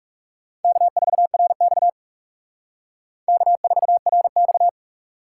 Morse code, heard as a keyed tone, about 39 wpm, 700 Hz. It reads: K4RX K4RX